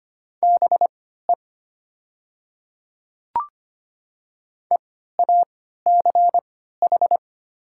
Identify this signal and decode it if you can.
Morse code, heard as a keyed tone, about 25 wpm, spaced as for 14 wpm, 700 Hz. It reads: BE EACH
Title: 3x Character Spacing - Top 100 Words - 25wpm